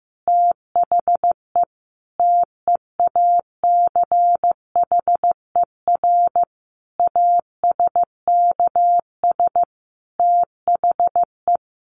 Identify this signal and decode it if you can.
Morse code, heard as a keyed tone, about 15 wpm, 700 Hz. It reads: THE TEACHER ASKS THE
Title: Additional Easy Phrases 15wpm